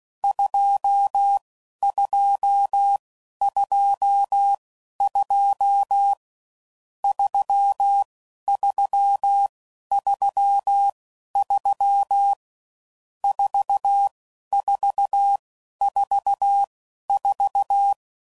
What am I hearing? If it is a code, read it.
Morse code, heard as a keyed tone, about 16 wpm, spaced as for 11 wpm, 800 Hz. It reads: 2222 3333 4444
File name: Mixed / Misc / Numbs